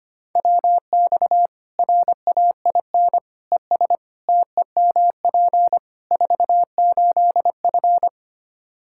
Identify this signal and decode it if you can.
Morse code, heard as a keyed tone, about 25 wpm, 700 Hz. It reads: WX RAIN ES TEMP 48F
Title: Practice QSO - Exchange 2 - Info - 25wpm